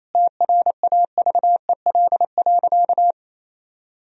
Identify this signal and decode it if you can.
Morse code, heard as a keyed tone, about 28 wpm, 700 Hz. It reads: TRAVEL.
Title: Sentences from Top 1000 Words - Review Part 2 - 28wpm